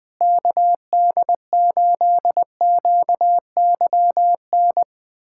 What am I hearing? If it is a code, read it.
Morse code, heard as a keyed tone, about 21 wpm, 700 Hz. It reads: KD8QYN
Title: USA Callsign DE Callsign - 6 Characters - 20wpm